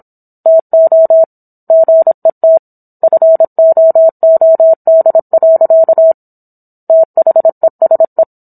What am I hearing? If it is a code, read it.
Morse code, heard as a keyed tone, about 27 wpm, 650 Hz. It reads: TO GET FOOD. THESE